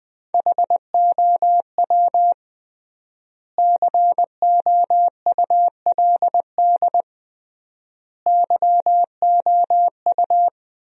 Morse code, heard as a keyed tone, about 21 wpm, 700 Hz. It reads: HOW COULD YOU
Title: 3x Word-Spacing - Sentences from Top 300 Words - 20wpm